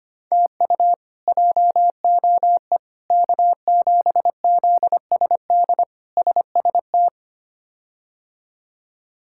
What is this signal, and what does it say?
Morse code, heard as a keyed tone, about 25 wpm, 700 Hz. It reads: TU JOE K7ZSD SST
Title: K1USN SST QSO Practice - 25wpm